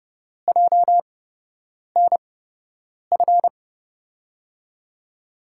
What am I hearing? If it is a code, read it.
Morse code, heard as a keyed tone, about 29 wpm, spaced as for 8 wpm, 700 Hz. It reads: JNF
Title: USA Call Signs - 6 Characters - 8x Character Spacing - 30wpm